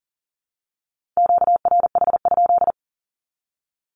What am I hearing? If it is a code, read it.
Morse code, heard as a keyed tone, about 40 wpm, 700 Hz. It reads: QRH?
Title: Q-Codes 40wpm